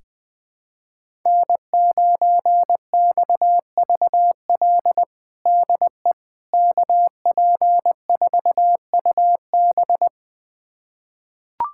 Morse code, heard as a keyed tone, about 21 wpm, 700 Hz. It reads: N9XVL DE KP4UB
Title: USA Callsign DE Callsign 20wpm